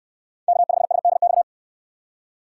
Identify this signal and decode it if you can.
Morse code, heard as a keyed tone, about 68 wpm, 700 Hz. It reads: K5SDX